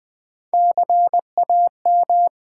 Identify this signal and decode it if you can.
Morse code, heard as a keyed tone, about 20 wpm, 700 Hz. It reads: CAM
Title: Build A Word - 3 Letters At A Time - 8-Letter Words - 20wpm